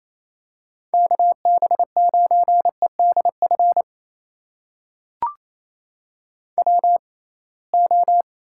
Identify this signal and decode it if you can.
Morse code, heard as a keyed tone, about 28 wpm, spaced as for 21 wpm, 700 Hz. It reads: KB9EDF WO
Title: USA Call Signs - 6x Character Spacing - 28wpm